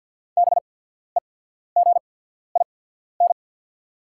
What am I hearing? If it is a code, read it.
Morse code, heard as a keyed tone, about 48 wpm, spaced as for 14 wpm, 690 Hz. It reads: BEGIN
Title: 8x Character Spacing - Top 300 Words - 50wpm